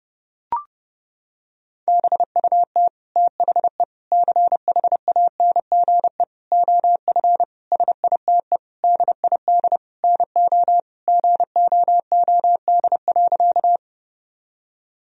Morse code, heard as a keyed tone, about 30 wpm, 700 Hz. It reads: BUT THE CHANGE OF SITE DID NO GOOD.
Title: Sentences from Top 1000 Words - Review Part 4 - 30wpm